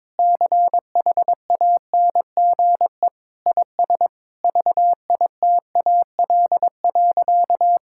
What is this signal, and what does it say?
Morse code, heard as a keyed tone, about 21 wpm, 700 Hz. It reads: CHANGE IS VITAL.
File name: practice / 22wpm / Sentences with 8-Word Phrases - 22wpm